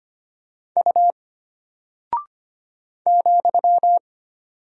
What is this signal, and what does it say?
Morse code, heard as a keyed tone, about 25 wpm, 700 Hz. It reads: U ,
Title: Single Character (Letter:Number:Punctuation:Prosign) Alt 25wpm